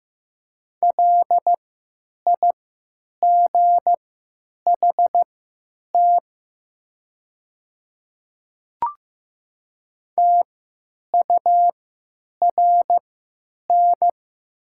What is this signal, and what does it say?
Morse code, heard as a keyed tone, about 15 wpm, spaced as for 9 wpm, 700 Hz. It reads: LIGHT TURN
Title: 3x Character Spacing - Top 200 Words - 15wpm